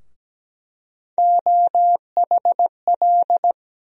Morse code, heard as a keyed tone, about 17 wpm, 700 Hz. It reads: OHL
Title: USA Call Sign Postfixes - 3 Letters 17wpm